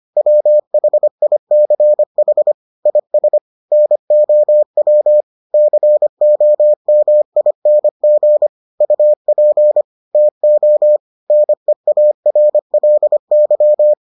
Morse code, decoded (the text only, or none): WHICH IS NOW COMING UP TO NEARLY